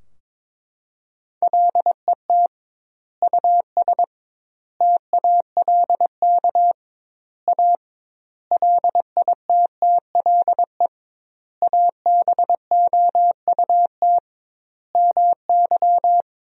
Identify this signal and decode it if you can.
Morse code, heard as a keyed tone, about 22 wpm, 700 Hz. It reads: LET US TALK A LITTLE ABOUT MY
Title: 2x Word-Spacing - Sentences from Top 400 Words - 22wpm